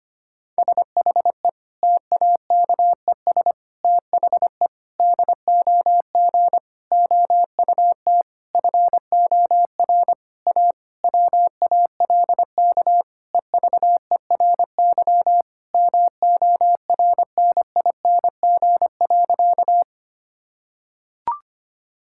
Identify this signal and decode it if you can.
Morse code, heard as a keyed tone, about 25 wpm, 700 Hz. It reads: SHE TAKES THE DOG OUT FOR A WALK EVERY MORNING.
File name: practice / 25wpm / Difficult Sentences with Hints - 25wpm